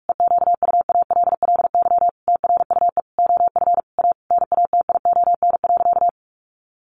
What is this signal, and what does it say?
Morse code, heard as a keyed tone, about 44 wpm, 700 Hz. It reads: EQUALLY TRUE OF A NATION.
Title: Sentences from Top 1400 Words 45wpm